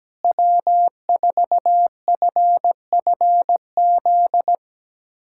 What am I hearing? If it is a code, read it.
Morse code, heard as a keyed tone, about 17 wpm, 700 Hz. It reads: W4FFZ
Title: USA Callsign DE Callsign - 5 Characters - 17wpm